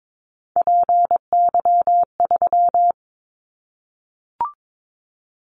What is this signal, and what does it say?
Morse code, heard as a keyed tone, about 22 wpm, 700 Hz. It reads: PY3
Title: Callsign Prefixes for Top 100 DXCC - 22wpm